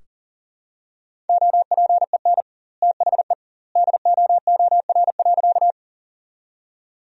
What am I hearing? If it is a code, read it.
Morse code, heard as a keyed tone, about 40 wpm, 700 Hz. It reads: OPEN THE DOOR.
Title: 2x Word-Spacing - Sentences from Top 300 Words - 40wpm